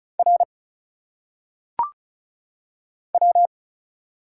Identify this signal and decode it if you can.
Morse code, heard as a keyed tone, about 36 wpm, 700 Hz. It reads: R W
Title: Single Letters - Phonetic Pronunciation 35wpm